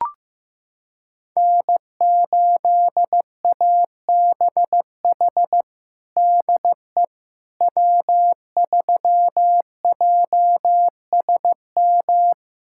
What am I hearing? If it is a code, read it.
Morse code, heard as a keyed tone, about 15 wpm, 700 Hz. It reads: N8ABH DE W3JSM
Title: USA Callsign DE Callsign - 5 Characters - 15wpm